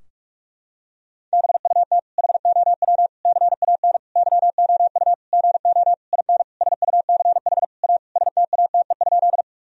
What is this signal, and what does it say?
Morse code, heard as a keyed tone, about 46 wpm, 700 Hz. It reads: BUT HOW CAN YOU GO IN SUCH A STATE?
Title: Sentences from Top 1000 Words - Review Part 2 - 45wpm